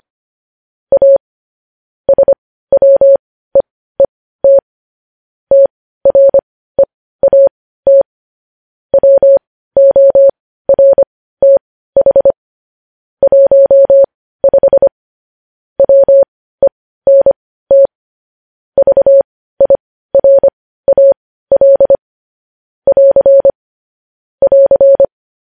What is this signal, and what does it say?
Morse code, heard as a keyed tone, about 25 wpm, spaced as for 15 wpm, 550 Hz. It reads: A SWEET TREAT WORTH 15 WENT VIRAL <AR> <AR>
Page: news-headlines-in-morse-code-at-15-wpm